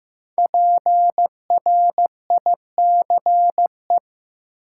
Morse code, heard as a keyed tone, about 15 wpm, 700 Hz. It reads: PRICE
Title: Five Letter Words - 15wpm